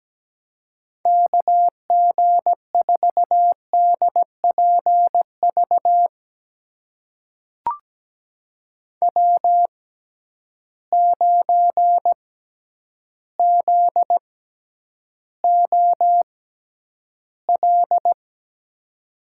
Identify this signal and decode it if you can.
Morse code, heard as a keyed tone, about 17 wpm, spaced as for 9 wpm, 700 Hz. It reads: KG4DPV W9ZOL
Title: USA Call Signs - 6x Character Spacing - 17wpm